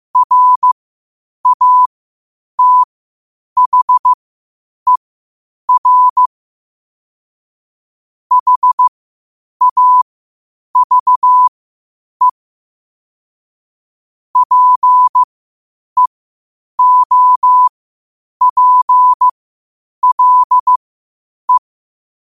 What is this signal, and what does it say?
Morse code, heard as a keyed tone, about 15 wpm, spaced as for 8 wpm, 1000 Hz. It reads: RATHER HAVE PEOPLE